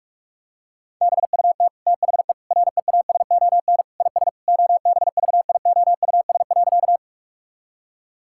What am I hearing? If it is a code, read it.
Morse code, heard as a keyed tone, about 46 wpm, 700 Hz. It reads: BUT THE REASON IS OBVIOUS.
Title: Sentences from Top 1600 Words 45wpm